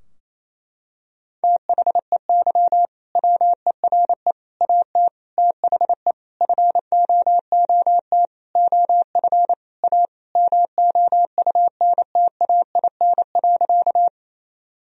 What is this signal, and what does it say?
Morse code, heard as a keyed tone, about 27 wpm, 700 Hz. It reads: THEY WERE AT THE FOOT OF A MOUNTAIN.
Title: Sentences from Top 1000 Words - Review Part 3 - 28wpm